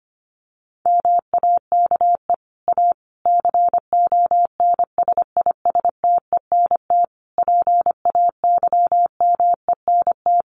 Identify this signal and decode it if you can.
Morse code, heard as a keyed tone, about 25 wpm, 700 Hz. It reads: MAKE A CONSISTENT PAYMENT
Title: Sentences from Top 1500 Words 25wpm